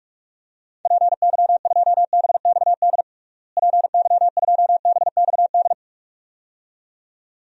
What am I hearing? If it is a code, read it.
Morse code, heard as a keyed tone, about 46 wpm, 700 Hz. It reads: PY2BXD PY2BXD
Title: DXCC-100 Call Signs - 6 Characters - Repeated 2x - 45wpm